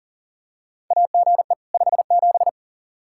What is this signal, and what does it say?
Morse code, heard as a keyed tone, about 40 wpm, 700 Hz. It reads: AGE 57
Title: Practice QSO - Exchange 3 - More Info - 40wpm